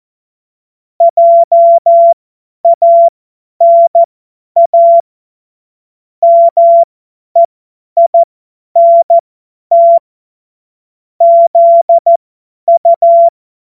Morse code, heard as a keyed tone, about 14 wpm, spaced as for 10 wpm, 680 Hz. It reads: JANA MEINT ZU
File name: Morsekurs / Witze / Witz 31. Januar 2023 50 F70